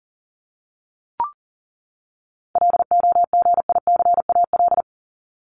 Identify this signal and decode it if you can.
Morse code, heard as a keyed tone, about 40 wpm, 700 Hz. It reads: LOGICAL
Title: Top 1501-1750 Words 40wpm